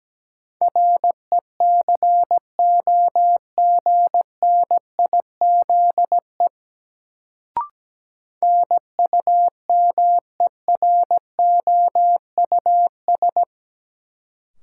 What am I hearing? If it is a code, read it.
Morse code, heard as a keyed tone, about 17 wpm, 700 Hz. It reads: RECOGNIZE NUMEROUS
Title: Sentences from Top 1400 Words - New Word Drill Alt - 17wpm